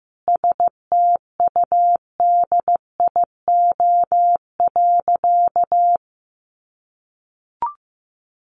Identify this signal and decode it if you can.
Morse code, heard as a keyed tone, about 15 wpm, 700 Hz. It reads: STUDIO.
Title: Sentences from Top 1200 Words 15wpm